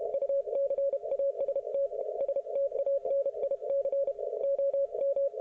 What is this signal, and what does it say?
Morse code, heard as a keyed tone, about 32 wpm, 550 Hz. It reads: UCAST STANIC OM